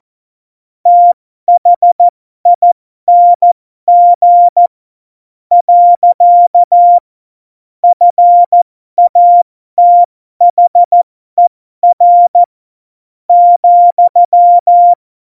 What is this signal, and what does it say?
Morse code, heard as a keyed tone, about 14 wpm, spaced as for 12 wpm, 700 Hz. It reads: THING . FATHER ,
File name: Morsekurs / Witze